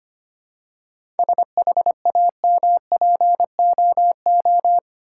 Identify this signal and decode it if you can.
Morse code, heard as a keyed tone, about 25 wpm, 690 Hz. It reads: SHAMPOO
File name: practice / 25wpm / Easy Words with Hints - 25wpm